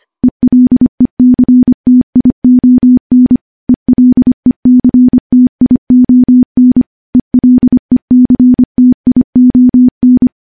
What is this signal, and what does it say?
Morse code, heard as a keyed tone, about 25 wpm, 250 Hz. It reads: ELECTION ELECTION ELECTION
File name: morse / morse-training-25WPM